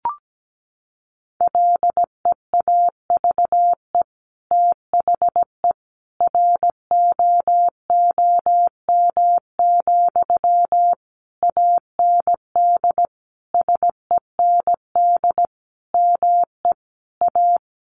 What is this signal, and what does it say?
Morse code, heard as a keyed tone, about 17 wpm, 700 Hz. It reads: LEAVE THE ROOM, AND SEND ME A
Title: Sentences from Top 1500 Words 17wpm